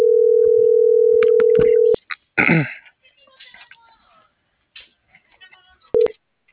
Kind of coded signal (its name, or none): telephone call progress tone